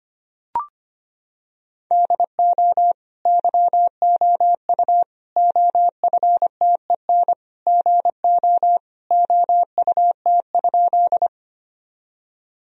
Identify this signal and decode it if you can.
Morse code, heard as a keyed tone, about 25 wpm, 700 Hz. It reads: DO YOU OFTEN GO OUT?
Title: Sentences from Top 1000 Words - Review Part 1 - 25wpm